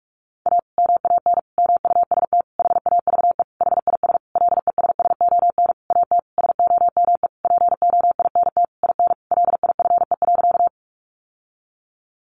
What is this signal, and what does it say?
Morse code, heard as a keyed tone, about 44 wpm, 700 Hz. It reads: A MAN MUST HAVE HIS LESSON AT SOME POINT IN LIFE.
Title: Sentences from Top 1200 Words 45wpm